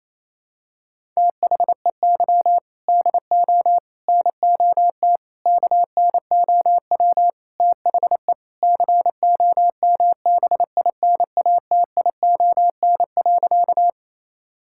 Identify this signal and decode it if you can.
Morse code, heard as a keyed tone, about 28 wpm, 700 Hz. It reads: THEY DO NOT KNOW THE COMBINATION.
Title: Sentences from Top 1000 Words - Review Part 6 - 28wpm